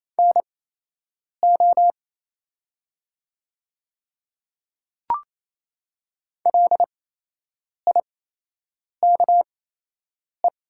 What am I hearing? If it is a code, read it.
Morse code, heard as a keyed tone, about 28 wpm, spaced as for 8 wpm, 700 Hz. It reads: NO LIKE
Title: 8x Character Spacing - Top 100 Words - 28wpm